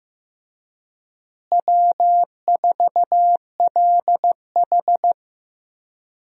Morse code, heard as a keyed tone, about 15 wpm, 700 Hz. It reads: W4LH
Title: USA Call Signs - 4 Characters - Repeated 2x - 15wpm